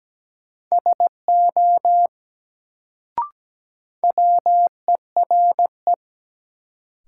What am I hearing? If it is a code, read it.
Morse code, heard as a keyed tone, about 17 wpm, 700 Hz. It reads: SO WERE
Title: Sets of 2 Words - New Word Drill - 17wpm